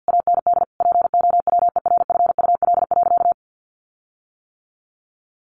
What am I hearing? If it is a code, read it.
Morse code, heard as a keyed tone, about 48 wpm, 700 Hz. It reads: AND POWERFUL.
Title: Sentences from Top 1300 Words 50wpm